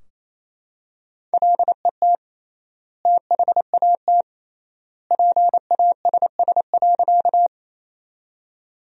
Morse code, heard as a keyed tone, about 28 wpm, 700 Hz. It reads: LET THAT PASS.
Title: 3x Word-Spacing - Sentences from Top 400 Words - 28wpm